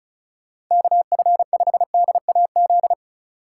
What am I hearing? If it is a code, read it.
Morse code, heard as a keyed tone, about 44 wpm, 700 Hz. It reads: KF5DAZ
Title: Speed Racing - USA 2x3 Call Signs 35wpm